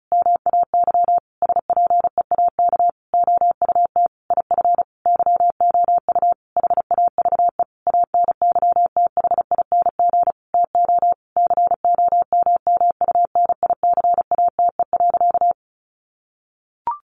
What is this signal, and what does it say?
Morse code, heard as a keyed tone, about 34 wpm, 700 Hz. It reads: MAY SPEAK OUT IF YOU HAVE ANYTHING TO COMMUNICATE.
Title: Sentences from Top 1300 Words 35wpm